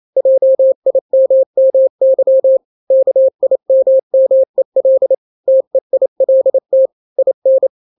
Morse code, heard as a keyed tone, about 28 wpm, 530 Hz. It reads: JIMMY KIMMEL TEILT IN
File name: daily / 27wpm